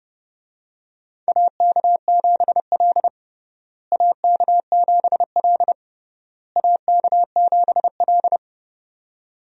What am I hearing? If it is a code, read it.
Morse code, heard as a keyed tone, about 30 wpm, 700 Hz. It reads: AK7L AK7L AK7L
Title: USA Call Signs - 4 Characters - Repeated 3x - 30wpm